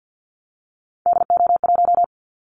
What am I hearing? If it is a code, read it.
Morse code, heard as a keyed tone, about 50 wpm, 700 Hz. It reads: DO1
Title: Callsign Prefixes for Top 25 DXCC - 50wpm